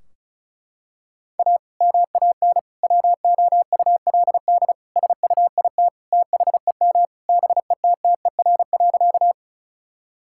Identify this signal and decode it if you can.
Morse code, heard as a keyed tone, about 36 wpm, 700 Hz. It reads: A MAN WOULD SUIT THEM BETTER.
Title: Sentences from Top 1600 Words 35wpm